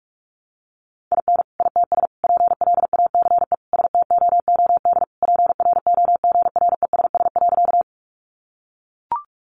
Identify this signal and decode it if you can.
Morse code, heard as a keyed tone, about 44 wpm, 700 Hz. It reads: IN ITS PLACE STOOD PROGRESS.
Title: Sentences from Top 1400 Words 45wpm